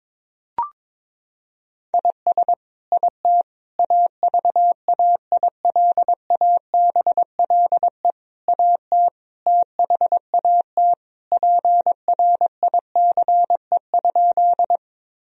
Morse code, heard as a keyed tone, about 22 wpm, 700 Hz. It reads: IS IT AVAILABLE AT THAT PRICE?
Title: Sentences from Top 1000 Words - Review Part 3 - 22wpm